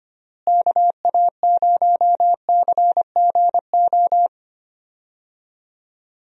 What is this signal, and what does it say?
Morse code, heard as a keyed tone, about 25 wpm, 700 Hz. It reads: KA0CGO
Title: USA Call Signs - 6 Characters - Repeated 3x - 25wpm